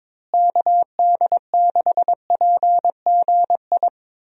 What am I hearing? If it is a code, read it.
Morse code, heard as a keyed tone, about 22 wpm, 700 Hz. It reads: KD6PGI
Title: USA Call Signs - 6 Characters - Repeated 3x - 22wpm